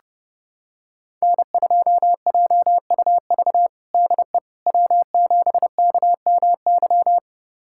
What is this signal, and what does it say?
Morse code, heard as a keyed tone, about 30 wpm, 700 Hz. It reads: N2JUV DE W7KMY
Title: USA Callsign DE Callsign - 5 Characters - 30wpm